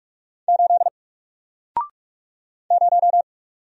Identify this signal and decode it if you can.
Morse code, heard as a keyed tone, about 45 wpm, 700 Hz. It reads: 8 0
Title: Single Character (Letter:Number:Punctuation:Prosign) Alt 45wpm